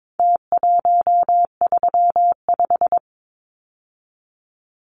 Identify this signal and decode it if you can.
Morse code, heard as a keyed tone, about 22 wpm, 700 Hz. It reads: T135
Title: Numbers - Contest Style - 4-Digit 22wpm